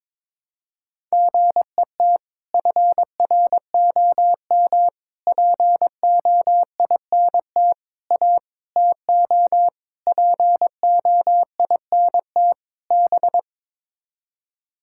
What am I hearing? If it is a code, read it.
Morse code, heard as a keyed tone, about 22 wpm, 700 Hz. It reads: GET FROM POINT A TO POINT B
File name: practice / 22wpm / Sets of 7 Words - 22wpm